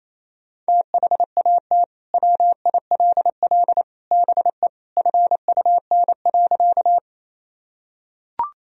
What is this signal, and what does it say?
Morse code, heard as a keyed tone, about 27 wpm, 700 Hz. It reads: THAT WILL BE FUN.
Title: Sentences from Top 1000 Words - Review Part 4 - 28wpm